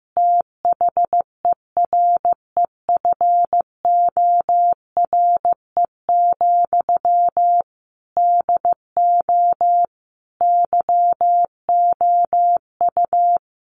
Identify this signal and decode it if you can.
Morse code, heard as a keyed tone, about 15 wpm, 700 Hz. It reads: THEREFORE, DO YOU
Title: Sentences from Top 1100 Words 15wpm